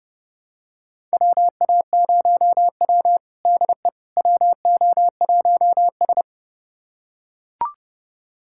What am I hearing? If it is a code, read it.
Morse code, heard as a keyed tone, about 30 wpm, 700 Hz. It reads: WA0W DE WO1S
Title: USA Callsign DE Callsign - 4 Characters - 30wpm